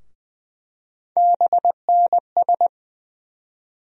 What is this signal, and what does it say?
Morse code, heard as a keyed tone, about 20 wpm, 700 Hz. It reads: BNS
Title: USA Call Sign Postfixes - 3 Letters 20wpm